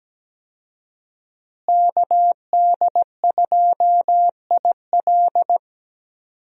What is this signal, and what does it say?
Morse code, heard as a keyed tone, about 17 wpm, 700 Hz. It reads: KD2IL
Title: USA Call Signs - 5 Characters - Repeated 2x - 17wpm